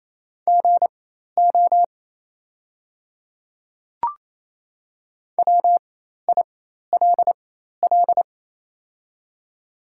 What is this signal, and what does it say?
Morse code, heard as a keyed tone, about 28 wpm, spaced as for 13 wpm, 700 Hz. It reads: GO WILL
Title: 4x Character Spacing - Top 100 Words - 28wpm